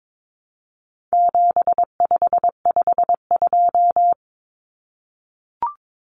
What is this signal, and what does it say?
Morse code, heard as a keyed tone, about 22 wpm, 700 Hz. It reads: 7552
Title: Common Cut Numbers - 4-Digit 22wpm